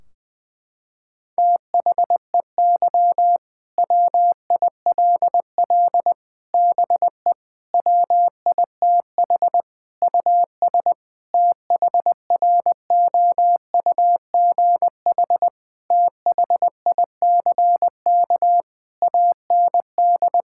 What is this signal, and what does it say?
Morse code, heard as a keyed tone, about 21 wpm, 700 Hz. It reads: THEY WILL BE WITH US THROUGH THICK AND
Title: Sentences from Top 1600 Words 20wpm